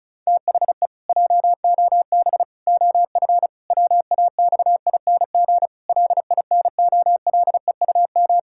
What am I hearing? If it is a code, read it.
Morse code, heard as a keyed tone, about 36 wpm, 700 Hz. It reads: THE JOB OF WAXING LINOLEUM
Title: Pangram Challenge v2 - 35wpm